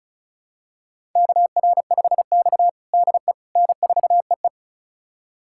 Morse code, heard as a keyed tone, about 34 wpm, 700 Hz. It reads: KR5X DE N4EE
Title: USA Callsign DE Callsign - 4 Characters - 35wpm